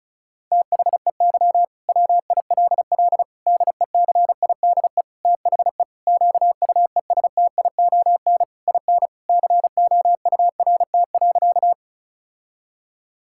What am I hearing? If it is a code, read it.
Morse code, heard as a keyed tone, about 36 wpm, 700 Hz. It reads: THEY WILL DECIDE THE QUESTION IN COURT.
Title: Sentences from Top 1600 Words 35wpm